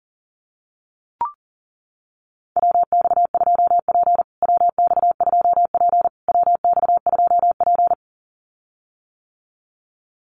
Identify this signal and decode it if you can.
Morse code, heard as a keyed tone, about 40 wpm, 700 Hz. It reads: WX2P WX2P WX2P
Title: USA Call Signs Repeated 3x 40wpm